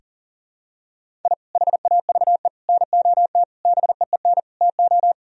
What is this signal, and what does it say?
Morse code, heard as a keyed tone, about 37 wpm, 700 Hz. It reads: I HAVE NOT BEEN TO
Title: Sentences from Top 1000 Words - Review Part 1 - 40wpm